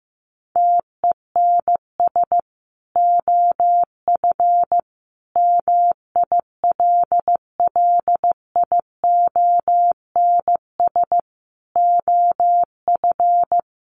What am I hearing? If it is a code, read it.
Morse code, heard as a keyed tone, about 15 wpm, 700 Hz. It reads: TENS OF MILLIONS OF
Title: Sets of 5 Words Encore 15wpm